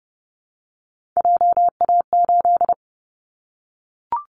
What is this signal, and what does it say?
Morse code, heard as a keyed tone, about 30 wpm, 700 Hz. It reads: JA8
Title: Callsign Prefixes for Top 25 DXCC - Repeated 3x - 30wpm